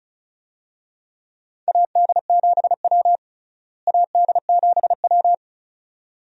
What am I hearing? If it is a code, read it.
Morse code, heard as a keyed tone, about 34 wpm, 700 Hz. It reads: AD7W AD7W
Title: USA Call Signs - 4 Characters - Repeated 2x - 35wpm